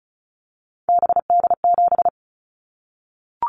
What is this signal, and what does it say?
Morse code, heard as a keyed tone, about 36 wpm, 700 Hz. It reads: BD7
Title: Callsign Prefixes for Top 100 DXCC - 35wpm